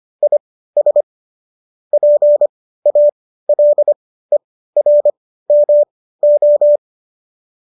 Morse code, heard as a keyed tone, about 25 wpm, spaced as for 15 wpm, 590 Hz. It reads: IS PALERMO